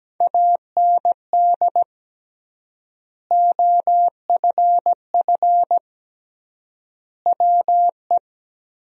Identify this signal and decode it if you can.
Morse code, heard as a keyed tone, about 17 wpm, 700 Hz. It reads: AND OFF WE
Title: 3x Word-Spacing - Sentences from Top 200 Words - 17wpm